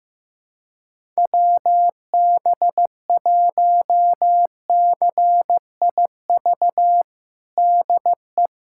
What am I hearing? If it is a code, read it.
Morse code, heard as a keyed tone, about 15 wpm, 700 Hz. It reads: WB1CIV DE
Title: USA Callsign DE Callsign - 6 Characters - 15wpm